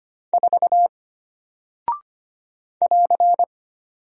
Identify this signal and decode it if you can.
Morse code, heard as a keyed tone, about 25 wpm, 700 Hz. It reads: 4 <AR>
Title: Single Character (Letter:Number:Punctuation:Prosign) Alt 25wpm